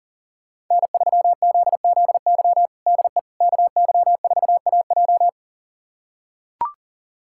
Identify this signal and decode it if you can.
Morse code, heard as a keyed tone, about 40 wpm, 700 Hz. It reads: N3ZZY DE KY4AJ
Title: USA Callsign DE Callsign - 5 Characters - 40wpm